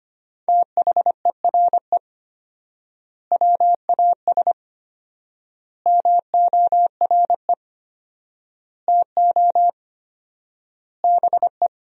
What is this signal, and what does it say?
Morse code, heard as a keyed tone, about 25 wpm, 700 Hz. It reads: THERE WAS MORE TO BE
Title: 4x Word-Spacing - Sentences from Top 500 Words - 25wpm